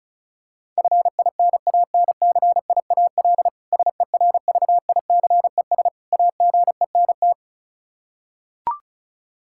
Morse code, heard as a keyed tone, about 34 wpm, 700 Hz. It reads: FINANCIAL SERVICES AGENT
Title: Practice QSO - Exchange 3 - More Info - 35wpm